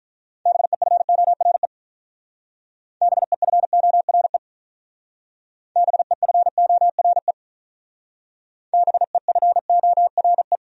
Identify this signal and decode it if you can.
Morse code, heard as a keyed tone, about 45 wpm, 700 Hz. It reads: BEFORE BEFORE BEFORE BEFORE